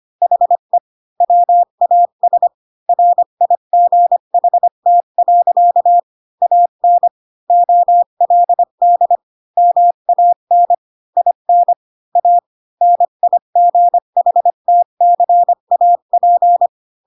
Morse code, heard as a keyed tone, about 25 wpm, 700 Hz. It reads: HE WAS RIGHT. AN OLD MAN IN A NIGHTCAP